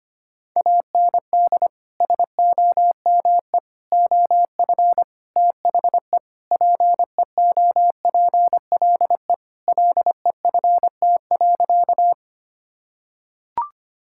Morse code, heard as a keyed tone, about 25 wpm, 700 Hz. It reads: AND SOME OF THE PEOPLE LEFT.
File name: practice / 25wpm / Sentences with 4-Word Phrases - 25wpm